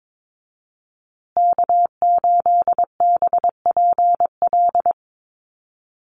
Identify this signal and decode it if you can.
Morse code, heard as a keyed tone, about 22 wpm, 700 Hz. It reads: K8BPL
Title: USA Call Signs Repeated 2x 22wpm